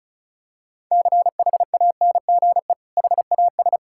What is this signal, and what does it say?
Morse code, heard as a keyed tone, about 36 wpm, 700 Hz. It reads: CHANGE HAS
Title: Sentences from Top 1600 Words 35wpm